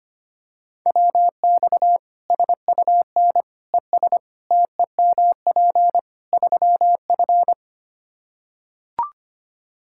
Morse code, heard as a keyed tone, about 25 wpm, 700 Hz. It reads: WX SUN ES TEMP 3F
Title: Practice QSO - Exchange 2 - Info - 25wpm